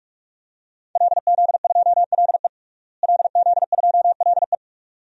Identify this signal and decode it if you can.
Morse code, heard as a keyed tone, about 46 wpm, 700 Hz. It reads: LZ2LE LZ2LE
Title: DXCC-100 Call Signs - 5 Characters - Repeated 2x - 45wpm